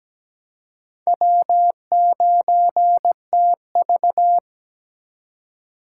Morse code, heard as a keyed tone, about 17 wpm, 700 Hz. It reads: W9TV
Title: USA Call Signs - 4 Characters - 2x Character Spacing - 17wpm